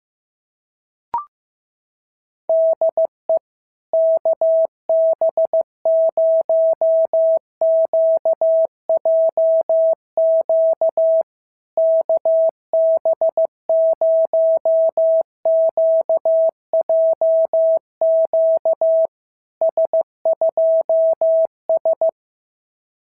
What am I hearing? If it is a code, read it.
Morse code, heard as a keyed tone, about 15 wpm, 650 Hz. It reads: DE KB0QJQ KB0QJQ S2S
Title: SOTA QSO Practice 15wpm